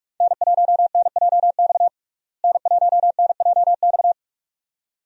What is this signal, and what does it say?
Morse code, heard as a keyed tone, about 46 wpm, 700 Hz. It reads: N1NJX N1NJX
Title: USA Call Signs - 5 Characters - Repeated 2x - 45wpm